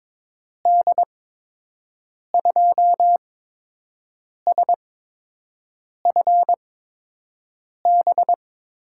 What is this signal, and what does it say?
Morse code, heard as a keyed tone, about 22 wpm, spaced as for 6 wpm, 700 Hz. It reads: D2SFB
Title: USA Call Signs - 6 Characters - 8x Character Spacing - 22wpm